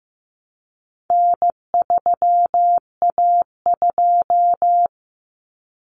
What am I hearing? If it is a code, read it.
Morse code, heard as a keyed tone, about 15 wpm, 700 Hz. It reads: N3A2
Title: Common Cut Numbers - 4-Digit 15wpm